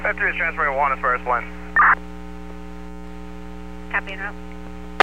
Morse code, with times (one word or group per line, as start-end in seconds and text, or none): none